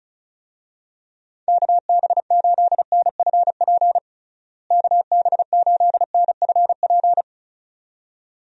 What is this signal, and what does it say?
Morse code, heard as a keyed tone, about 34 wpm, 700 Hz. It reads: KB8NFP KB8NFP
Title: USA Call Signs - 6 Characters - Repeated 2x - 35wpm